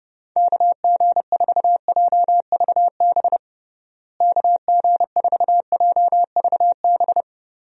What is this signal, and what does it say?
Morse code, heard as a keyed tone, about 30 wpm, 700 Hz. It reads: KG4JVB KG4JVB
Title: USA Call Signs - 6 Characters - Repeated 2x - 30wpm